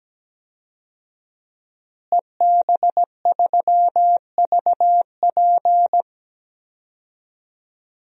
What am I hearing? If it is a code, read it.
Morse code, heard as a keyed tone, about 17 wpm, 700 Hz. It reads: EB3VP
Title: DXCC-100 Call Signs - 5 Characters - Repeated 3x - 17wpm